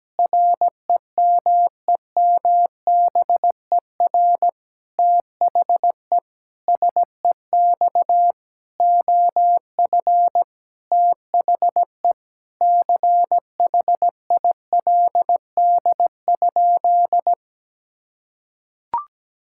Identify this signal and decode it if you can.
Morse code, heard as a keyed tone, about 17 wpm, 700 Hz. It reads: REMEMBER THE SEX OF THE CHILD?
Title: Sentences from Top 1600 Words 17wpm